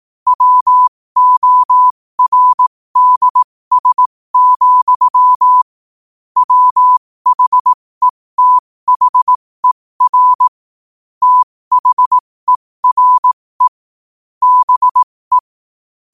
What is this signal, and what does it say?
Morse code, heard as a keyed tone, about 18 wpm, spaced as for 15 wpm, 1000 Hz. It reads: WORDS, WHETHER THERE BE
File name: qotdfiles / 15wpm